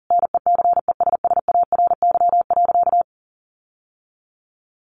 Morse code, heard as a keyed tone, about 40 wpm, 700 Hz. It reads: NECESSARY.
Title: Sentences from Top 1500 Words 40wpm